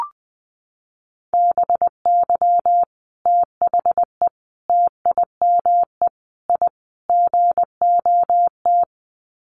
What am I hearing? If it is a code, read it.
Morse code, heard as a keyed tone, about 20 wpm, 700 Hz. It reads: BY THE TIME I GOT
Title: Sets of 5 Words Encore2 20wpm